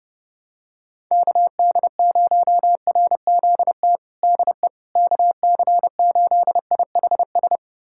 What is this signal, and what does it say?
Morse code, heard as a keyed tone, about 32 wpm, 700 Hz. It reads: KD0RZT DE KC8IHS
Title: USA Callsign DE Callsign 30wpm